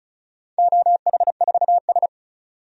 Morse code, heard as a keyed tone, about 34 wpm, 700 Hz. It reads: OH4S
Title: DXCC-100 Call Signs - 4 Characters - 35wpm